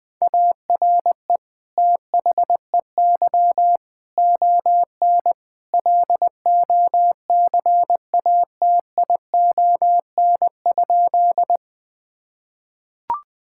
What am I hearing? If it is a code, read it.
Morse code, heard as a keyed tone, about 19 wpm, 700 Hz. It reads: ARE THEY ON LOCATION?
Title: Sentences from Top 1000 Words - Review Part 6 - 20wpm